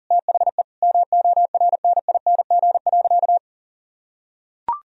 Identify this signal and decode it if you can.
Morse code, heard as a keyed tone, about 37 wpm, 700 Hz. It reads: THE MORNING.
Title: Sentences from Top 1000 Words - Review Part 4 - 40wpm